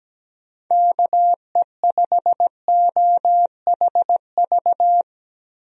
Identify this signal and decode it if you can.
Morse code, heard as a keyed tone, about 17 wpm, 700 Hz. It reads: KE5OHV
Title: USA Call Signs - 6 Characters - Repeated 3x - 17wpm